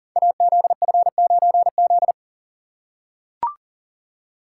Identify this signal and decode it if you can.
Morse code, heard as a keyed tone, about 40 wpm, 690 Hz. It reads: AZF9Z
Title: Code Groups - 5 Letter:Numbers - 40wpm